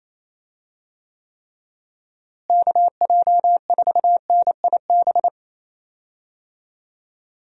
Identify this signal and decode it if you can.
Morse code, heard as a keyed tone, about 28 wpm, 700 Hz. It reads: KJ4NIB
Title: USA Call Signs - 6 Characters - Repeated 3x - 28wpm